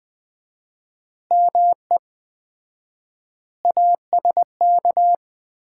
Morse code, heard as a keyed tone, about 20 wpm, 700 Hz. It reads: ME ASK
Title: 4x Word-Spacing - Sets of 4 Words - 20wpm